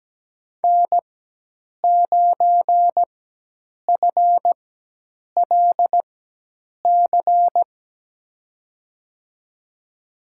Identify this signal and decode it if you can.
Morse code, heard as a keyed tone, about 17 wpm, spaced as for 8 wpm, 700 Hz. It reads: N9FLC
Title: USA Call Signs - 5 Characters - 4x Character Spacing - 17wpm